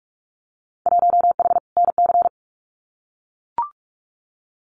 Morse code, heard as a keyed tone, about 44 wpm, 700 Hz. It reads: 1H NC
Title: Winter Field Day Exchanges 45wpm